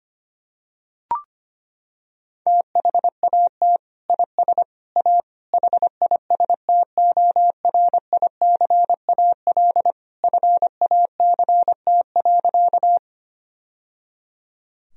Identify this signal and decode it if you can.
Morse code, heard as a keyed tone, about 25 wpm, 700 Hz. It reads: THAT IS A HISTORICAL FACT.
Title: Sentences from Top 1000 Words - Review Part 4 - 25wpm